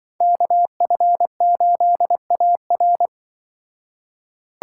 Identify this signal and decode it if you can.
Morse code, heard as a keyed tone, about 25 wpm, 700 Hz. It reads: KF8AR